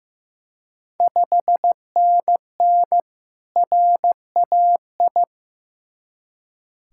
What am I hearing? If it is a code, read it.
Morse code, heard as a keyed tone, about 16 wpm, 700 Hz. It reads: 5NN RAI
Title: Ontario QSO Party Exchange 15wpm